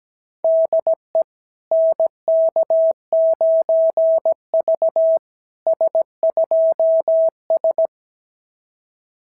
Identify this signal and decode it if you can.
Morse code, heard as a keyed tone, about 17 wpm, 660 Hz. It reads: DE NK9V S2S
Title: SOTA QSO Practice 17wpm